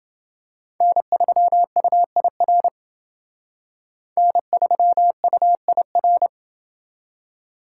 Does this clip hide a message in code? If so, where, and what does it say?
Yes: Morse code from 0.8 to 6.3 s, heard as a keyed tone, about 29 wpm, 700 Hz. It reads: N3UIR N3UIR